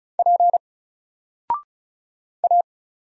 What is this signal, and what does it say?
Morse code, heard as a keyed tone, about 33 wpm, 700 Hz. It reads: P A
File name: Single-Letter-Number-35wpm